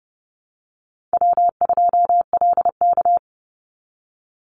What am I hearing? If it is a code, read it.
Morse code, heard as a keyed tone, about 30 wpm, 700 Hz. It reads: W2LK
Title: Easy Call Signs - 30wpm